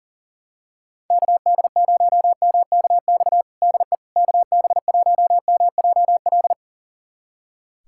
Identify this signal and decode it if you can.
Morse code, heard as a keyed tone, about 40 wpm, 700 Hz. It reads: KD0MKX DE KB1MJL
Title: USA Callsign DE Callsign - 6 Characters - 40wpm